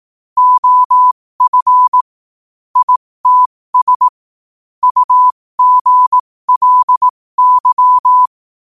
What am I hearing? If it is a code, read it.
Morse code, heard as a keyed tone, about 18 wpm, spaced as for 15 wpm, 1000 Hz. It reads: OF ITS UGLY
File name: qotdfiles / 15wpm